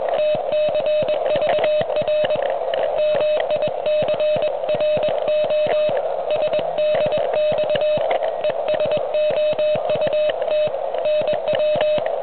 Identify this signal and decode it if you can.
Morse code, heard as a keyed tone, about 22 wpm, 610 Hz, with noise. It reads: TCVR MICRO SDX ES OUT NW